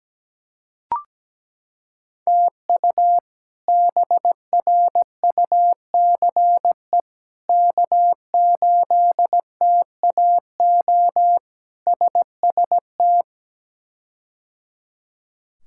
Morse code, heard as a keyed tone, about 17 wpm, 700 Hz. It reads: TU BRUCE K8TAO SST
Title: K1USN SST QSO Practice - 17wpm